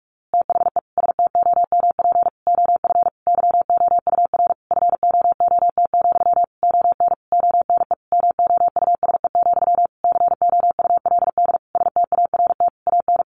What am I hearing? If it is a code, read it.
Morse code, heard as a keyed tone, about 44 wpm, 700 Hz. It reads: THE STOMP OF YOUR FOOT, ON ONE MOUSE, COULD START AN
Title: A Sound Of Thunder 45wpm